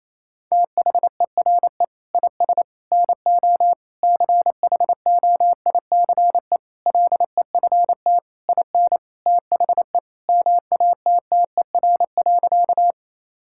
Morse code, heard as a keyed tone, about 28 wpm, 700 Hz. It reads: THERE IS NO CHOICE LEFT IN THE MATTER.
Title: Sentences from Top 1000 Words - Review Part 6 - 28wpm